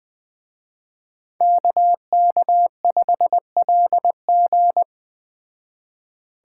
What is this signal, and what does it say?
Morse code, heard as a keyed tone, about 20 wpm, 700 Hz. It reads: KK5LG
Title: Speed Racing - USA 2x2 Call Signs 17wpm